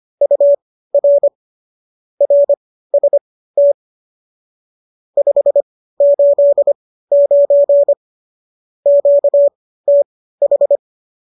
Morse code, heard as a keyed tone, about 25 wpm, spaced as for 15 wpm, 570 Hz. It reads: UR RST 589 QTH
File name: qso211 15